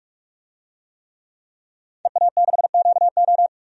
Morse code, heard as a keyed tone, about 46 wpm, 700 Hz. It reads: EA6QO
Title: DXCC-100 Call Signs - 5 Characters - 45wpm